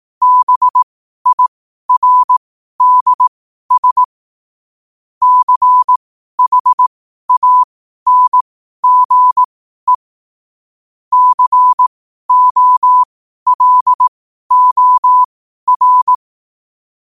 Morse code, heard as a keyed tone, about 18 wpm, spaced as for 12 wpm, 1000 Hz. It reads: BIRDS CHANGE COLOR